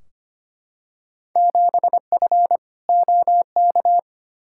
Morse code, heard as a keyed tone, about 25 wpm, 700 Hz. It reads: 7F OK
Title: Field Day Exchanges Encore - 25wpm